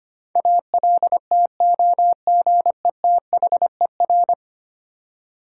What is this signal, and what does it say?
Morse code, heard as a keyed tone, about 25 wpm, 700 Hz. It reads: ALTOGETHER
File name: 10 Letter Words - First 3 Letters Given - 25wpm